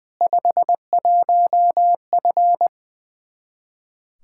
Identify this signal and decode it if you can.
Morse code, heard as a keyed tone, about 20 wpm, 700 Hz. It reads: 51F
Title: Practice QSO - Exchange 2 - Info - 20wpm